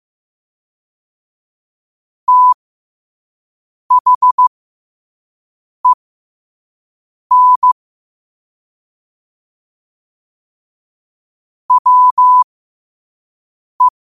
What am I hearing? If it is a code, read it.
Morse code, heard as a keyed tone, about 15 wpm, spaced as for 5 wpm, 1000 Hz. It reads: THEN WE
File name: qotdfiles / 5wpm